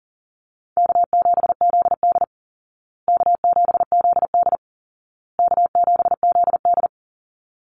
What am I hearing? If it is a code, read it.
Morse code, heard as a keyed tone, about 40 wpm, 700 Hz. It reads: K7ZD K7ZD K7ZD
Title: Easy Call Signs Repeated 3x - 40wpm